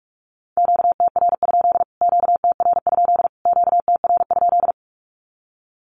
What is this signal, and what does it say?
Morse code, heard as a keyed tone, about 44 wpm, 700 Hz. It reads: QTR? QTR? QTR?
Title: Q-Codes Repeated 3x 45wpm